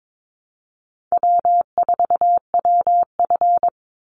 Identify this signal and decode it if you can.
Morse code, heard as a keyed tone, about 22 wpm, 700 Hz. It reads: W4WF
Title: Call Signs - 22wpm